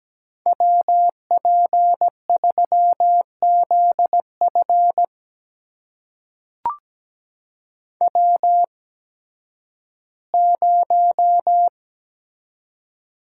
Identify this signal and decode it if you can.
Morse code, heard as a keyed tone, about 17 wpm, spaced as for 11 wpm, 700 Hz. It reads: WP3ZF W0
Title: USA Call Signs - 5 Characters - 8x Character Spacing - 17wpm